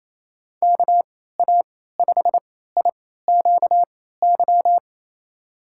Morse code, heard as a keyed tone, about 28 wpm, spaced as for 16 wpm, 700 Hz. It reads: KA5IQY
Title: USA Call Signs - 6 Characters - 3x Character Spacing - 28wpm